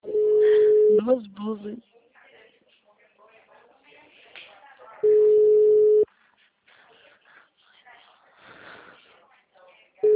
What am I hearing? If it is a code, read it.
Ringback tone (Europe, 425 Hz), with speech.
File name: out- -2022- - - .7286